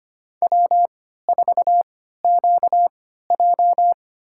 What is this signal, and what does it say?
Morse code, heard as a keyed tone, about 25 wpm, spaced as for 14 wpm, 700 Hz. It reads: W4QJ
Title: USA Call Signs - 4 Characters - 3x Character Spacing - 25wpm